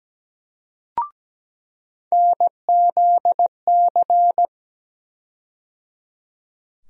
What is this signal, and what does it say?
Morse code, heard as a keyed tone, about 17 wpm, 700 Hz. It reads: NZC
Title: USA Call Sign Postfixes - 3 Letters 17wpm